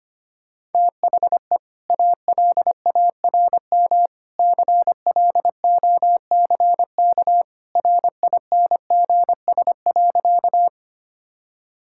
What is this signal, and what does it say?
Morse code, heard as a keyed tone, about 25 wpm, 700 Hz. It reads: THE ALARM CLOCK RINGS.
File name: Easy Sentences with Hints - 25wpm